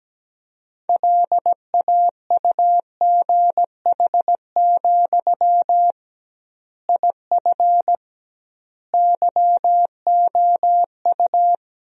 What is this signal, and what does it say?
Morse code, heard as a keyed tone, about 17 wpm, 700 Hz. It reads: LAUGH, IF YOU
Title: 2x Word-Spacing - Sentences from Top 500 Words - 17wpm